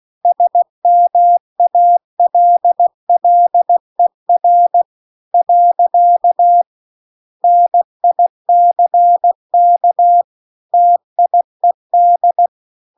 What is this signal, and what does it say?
Morse code, heard as a keyed tone, about 16 wpm, 700 Hz. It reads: SMALLER . NICK TIED